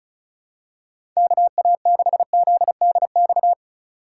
Morse code, heard as a keyed tone, about 34 wpm, 700 Hz. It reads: KA6ZDX
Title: USA Call Signs - 6 Characters - 4x Character Spacing - 35wpm